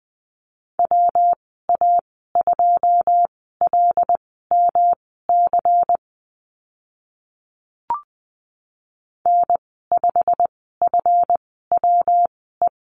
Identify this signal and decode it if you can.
Morse code, heard as a keyed tone, about 20 wpm, spaced as for 14 wpm, 700 Hz. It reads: WA2LMC N5FWE
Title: 147 Lesson 24g - Letter y - Call Signs 20wpm